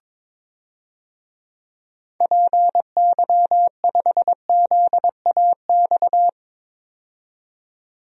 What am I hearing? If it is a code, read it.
Morse code, heard as a keyed tone, about 22 wpm, 700 Hz. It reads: PY5ZAX
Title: DXCC-100 Call Signs - 6 Characters - 22wpm